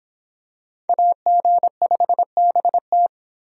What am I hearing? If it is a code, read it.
Morse code, heard as a keyed tone, about 25 wpm, 700 Hz. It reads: AG5BT